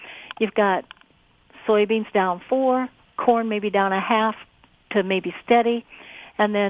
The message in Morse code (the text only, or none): none